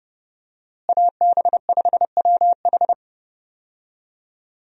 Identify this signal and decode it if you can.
Morse code, heard as a keyed tone, about 32 wpm, 700 Hz. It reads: AB5WH
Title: USA Call Signs - 5 Characters - Repeated 2x - 30wpm